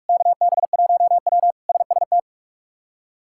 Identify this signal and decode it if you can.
Morse code, heard as a keyed tone, about 44 wpm, 700 Hz. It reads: KB1W SST
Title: K1USN SST QSO Practice - 45wpm